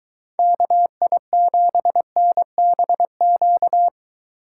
Morse code, heard as a keyed tone, about 22 wpm, 700 Hz. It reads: KI7NBQ